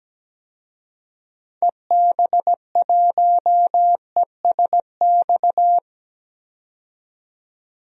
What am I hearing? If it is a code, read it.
Morse code, heard as a keyed tone, about 17 wpm, 700 Hz. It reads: EB1ESX
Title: DXCC-100 Call Signs - 6 Characters - 17wpm